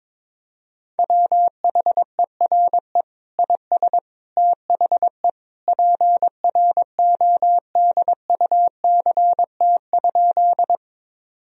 Morse code, heard as a keyed tone, about 22 wpm, 700 Hz. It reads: WHERE IS THE PRODUCT?
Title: Sentences from Top 1000 Words - Review Part 2 - 22wpm